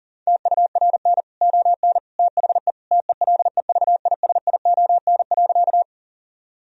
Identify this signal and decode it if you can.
Morse code, heard as a keyed tone, about 37 wpm, 700 Hz. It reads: TURN ON THE TELEVISION.
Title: Sentences from Top 1000 Words - Review Part 4 - 40wpm